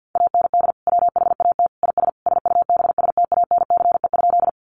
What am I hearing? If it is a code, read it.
Morse code, heard as a keyed tone, about 50 wpm, 700 Hz. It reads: AND WHAT IS SUBSTANCE?
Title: Sentences from Top 1500 Words 50wpm